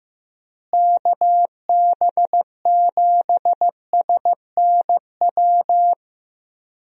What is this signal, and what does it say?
Morse code, heard as a keyed tone, about 15 wpm, 700 Hz. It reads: KB7SNW
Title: USA Call Signs - 6 Characters - Repeated 3x - 15wpm